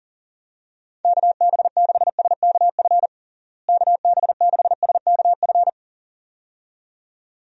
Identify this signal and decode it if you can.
Morse code, heard as a keyed tone, about 40 wpm, 700 Hz. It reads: KB6SKF KB6SKF
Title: USA Call Signs - 6 Characters - Repeated 2x - 40wpm